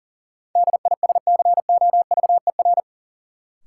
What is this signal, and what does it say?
Morse code, heard as a keyed tone, about 40 wpm, 700 Hz. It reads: DISCOVER
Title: Sentences from Top 1100 Words - New Word Drill - 40wpm